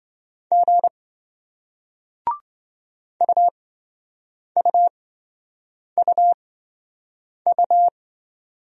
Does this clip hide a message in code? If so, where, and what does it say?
Yes: Morse code from 0.5 to 7.9 s, heard as a keyed tone, about 27 wpm, spaced as for 7 wpm, 700 Hz. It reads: G UUUU